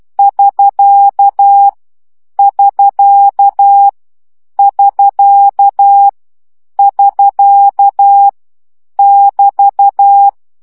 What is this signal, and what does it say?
Morse code, heard as a keyed tone, about 12 wpm, 800 Hz. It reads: <SK> <SK> <SK> <SK> <BT>
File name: CW gyakorlas sk 10 baud